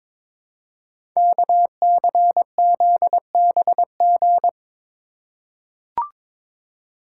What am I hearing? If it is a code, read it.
Morse code, heard as a keyed tone, about 22 wpm, 700 Hz. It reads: KCZBG
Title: Code Groups - 5 Letters - 22wpm